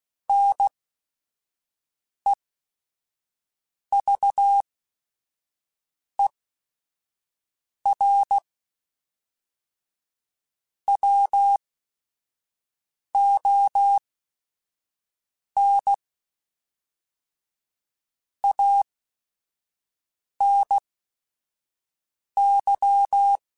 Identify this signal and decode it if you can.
Morse code, heard as a keyed tone, about 16 wpm, spaced as for 5 wpm, 800 Hz. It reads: NEVER WON ANY
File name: Quotes / Quotes-mp3s / Pessimism @5